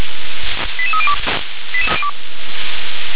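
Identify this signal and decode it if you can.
Five-tone selective call (ZVEI), heard as quick runs of five tones, tones of about 80 ms. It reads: 88302
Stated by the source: ZVEI five-tone selective call received via SDR.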